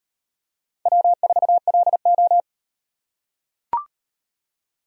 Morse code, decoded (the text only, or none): W4LO